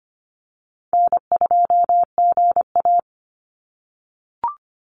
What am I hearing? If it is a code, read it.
Morse code, heard as a keyed tone, about 25 wpm, 700 Hz. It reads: N2GA
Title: Call Signs - 25wpm